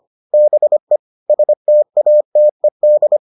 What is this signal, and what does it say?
Morse code, heard as a keyed tone, about 25 wpm, 600 Hz. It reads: BE STATED